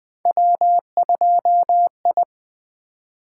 Morse code, heard as a keyed tone, about 24 wpm, 700 Hz. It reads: W2I
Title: Speed-Racing - Three Letter-Number Combinations - 20wpm